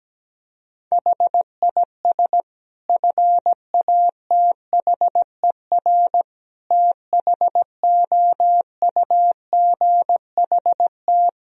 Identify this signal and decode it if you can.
Morse code, heard as a keyed tone, about 17 wpm, 700 Hz. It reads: HIS FATHER THOUGHT
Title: Sentences from Top 1000 Words - Review Part 4 - 17wpm